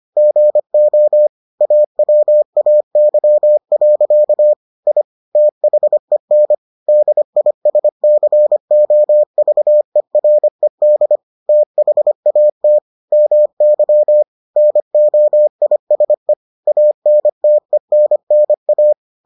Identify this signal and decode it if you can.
Morse code, heard as a keyed tone, about 25 wpm, 600 Hz. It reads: GO AWAY. I THEN DISCOVERED THAT MY NOISE ANTENNA